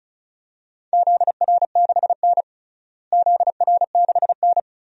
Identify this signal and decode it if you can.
Morse code, heard as a keyed tone, about 36 wpm, 700 Hz. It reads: ZR6N ZR6N
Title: DXCC-100 Call Signs - 4 Characters - Repeated 3x - 35wpm